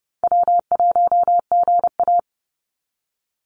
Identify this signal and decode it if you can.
Morse code, heard as a keyed tone, about 30 wpm, 700 Hz. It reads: W1GA
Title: USA Call Signs 30wpm